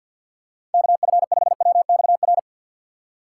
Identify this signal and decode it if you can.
Morse code, heard as a keyed tone, about 50 wpm, 700 Hz. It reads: KF5WXR
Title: USA Call Signs - 6 Characters - 6x Character Spacing - 50wpm